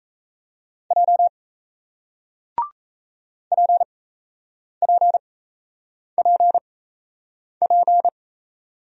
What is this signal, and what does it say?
Morse code, heard as a keyed tone, about 37 wpm, spaced as for 8 wpm, 700 Hz. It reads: J PPPP